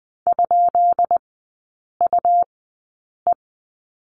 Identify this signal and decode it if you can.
Morse code, heard as a keyed tone, about 20 wpm, spaced as for 8 wpm, 690 Hz. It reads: ?UE
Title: 198 Lesson 32e - Letter k - Three Characters 20wpm